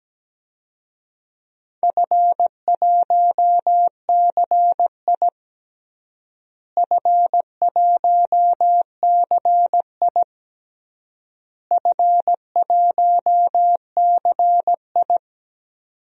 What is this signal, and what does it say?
Morse code, heard as a keyed tone, about 17 wpm, 700 Hz. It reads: F1CI F1CI F1CI
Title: DXCC-100 Call Signs - 4 Characters - Repeated 3x - 17wpm